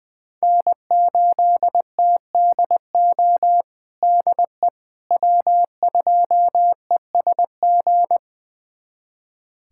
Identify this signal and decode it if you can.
Morse code, heard as a keyed tone, about 20 wpm, 700 Hz. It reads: N8TDO DE W2ESG
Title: USA Callsign DE Callsign - 5 Characters - 20wpm